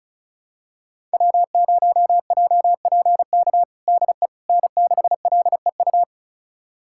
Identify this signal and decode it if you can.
Morse code, heard as a keyed tone, about 34 wpm, 700 Hz. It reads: W0JPK DE N6LEU
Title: USA Callsign DE Callsign - 5 Characters - 35wpm